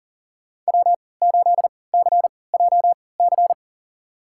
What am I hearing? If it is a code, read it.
Morse code, heard as a keyed tone, about 42 wpm, spaced as for 23 wpm, 690 Hz. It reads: W8CJC
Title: USA Call Signs - 5 Characters - 3x Character Spacing - 40wpm